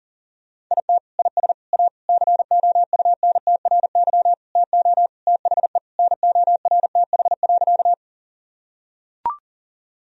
Morse code, heard as a keyed tone, about 37 wpm, 700 Hz. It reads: IT IS A COUNTRY TO THE NORTH.
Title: Sentences from Top 1000 Words - Review Part 1 - 40wpm